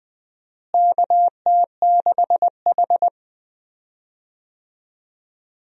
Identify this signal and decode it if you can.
Morse code, heard as a keyed tone, about 21 wpm, 700 Hz. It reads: KT6H
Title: USA Call Signs - 4 Characters - Repeated 2x - 20wpm